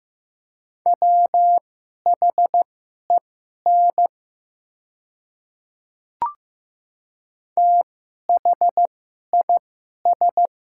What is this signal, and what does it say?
Morse code, heard as a keyed tone, about 15 wpm, spaced as for 11 wpm, 700 Hz. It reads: WHEN THIS
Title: 2x Character Spacing - Top 200 Words - 15wpm